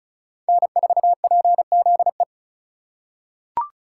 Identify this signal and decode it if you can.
Morse code, heard as a keyed tone, about 36 wpm, 700 Hz. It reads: N4PZE
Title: USA Call Signs - 4x Character Spacing - 35wpm